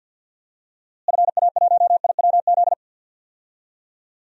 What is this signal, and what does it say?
Morse code, heard as a keyed tone, about 52 wpm, 700 Hz. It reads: FA1IWZ
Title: DXCC-100 Call Signs - 6 Characters - Repeated 3x - 50wpm